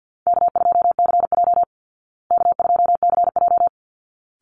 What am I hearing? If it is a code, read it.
Morse code, heard as a keyed tone, about 50 wpm, 700 Hz. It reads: K2CJ K2CJ
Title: Call Signs Repeated 3x - 50wpm